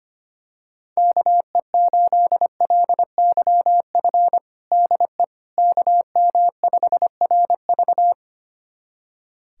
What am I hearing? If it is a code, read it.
Morse code, heard as a keyed tone, about 25 wpm, 700 Hz. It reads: KE8LYF DE KM5RV
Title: USA Callsign DE Callsign 25wpm